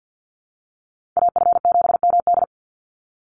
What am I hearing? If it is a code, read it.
Morse code, heard as a keyed tone, about 49 wpm, 700 Hz. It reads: AF7MD
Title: NA SOTA Chaser Callsigns 50wpm